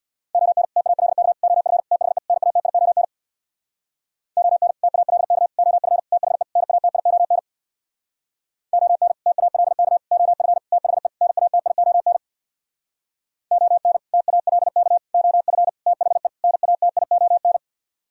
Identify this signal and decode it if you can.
Morse code, heard as a keyed tone, about 64 wpm, 700 Hz. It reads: ON TALK OF THE NATION ON TALK OF THE NATION ON TALK OF THE NATION ON TALK OF THE NATION